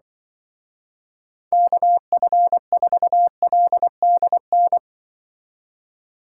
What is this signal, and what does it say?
Morse code, heard as a keyed tone, about 24 wpm, 700 Hz. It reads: KF4LDN
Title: Speed Racing - USA 2x3 Call Signs 20wpm